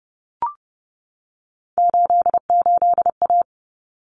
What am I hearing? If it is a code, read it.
Morse code, heard as a keyed tone, about 30 wpm, 700 Hz. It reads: 88A
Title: Common Cut Numbers - 3-Digit 30wpm